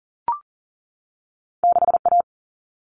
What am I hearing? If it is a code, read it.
Morse code, heard as a keyed tone, about 40 wpm, 700 Hz. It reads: 6A
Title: Common Cut Numbers - 2-Digit 40wpm